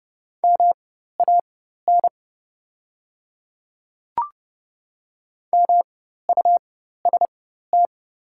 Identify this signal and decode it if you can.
Morse code, heard as a keyed tone, about 30 wpm, spaced as for 14 wpm, 700 Hz. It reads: MAN MUST
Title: 4x Character Spacing - Top 200 Words - 30wpm